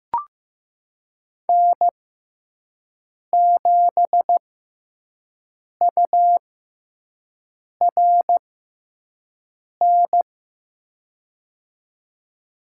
Morse code, heard as a keyed tone, about 15 wpm, spaced as for 5 wpm, 700 Hz. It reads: N7URN
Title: USA Call Signs - 5 Characters - 6x Character Spacing - 15wpm